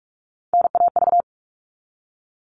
Morse code, heard as a keyed tone, about 44 wpm, 700 Hz. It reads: NAV
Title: All Cut Numbers - 3-Digit 45wpm